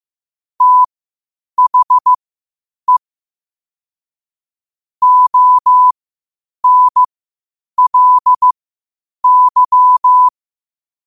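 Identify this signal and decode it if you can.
Morse code, heard as a keyed tone, about 15 wpm, spaced as for 8 wpm, 1000 Hz. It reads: THE ONLY
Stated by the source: Morse practice at 8 WPM from Morse Resource.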